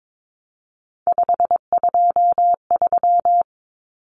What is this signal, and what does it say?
Morse code, heard as a keyed tone, about 22 wpm, 700 Hz. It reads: 523
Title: Numbers - 3-Digit 22wpm